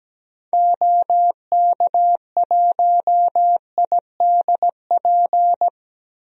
Morse code, heard as a keyed tone, about 17 wpm, 700 Hz. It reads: OK1IDP
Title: DXCC-100 Call Signs - 6 Characters - 17wpm